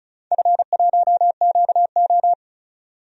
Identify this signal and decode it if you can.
Morse code, heard as a keyed tone, about 34 wpm, 700 Hz. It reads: F1QO
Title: DXCC-100 Call Signs - 4 Characters - 35wpm